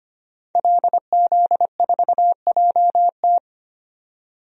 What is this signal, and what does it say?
Morse code, heard as a keyed tone, about 25 wpm, 700 Hz. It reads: LZ4JT
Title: DXCC-100 Call Signs - 5 Characters - 25wpm